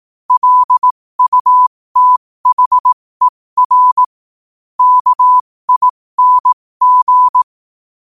Morse code, heard as a keyed tone, about 18 wpm, spaced as for 15 wpm, 1000 Hz. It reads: LUTHER KING
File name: qotdfiles / 15wpm